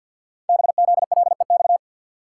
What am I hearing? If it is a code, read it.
Morse code, heard as a keyed tone, about 49 wpm, 700 Hz. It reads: BZLEX
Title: Code Groups - 5 Letters - 50wpm